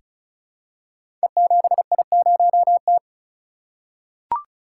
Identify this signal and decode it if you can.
Morse code, heard as a keyed tone, about 36 wpm, 700 Hz. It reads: E7I0T
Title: Code Groups - 5 Letter:Numbers - 35wpm